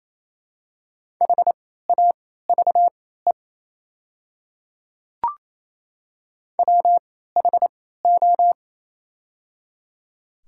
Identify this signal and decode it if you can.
Morse code, heard as a keyed tone, about 28 wpm, spaced as for 16 wpm, 690 Hz. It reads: HAVE WHO
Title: 3x Character Spacing - Top 100 Words - 28wpm